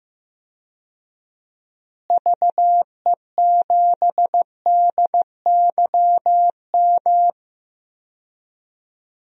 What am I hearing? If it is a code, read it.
Morse code, heard as a keyed tone, about 15 wpm, 700 Hz. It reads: VE7DYM
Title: DXCC-100 Call Signs - 6 Characters - Repeated 2x - 15wpm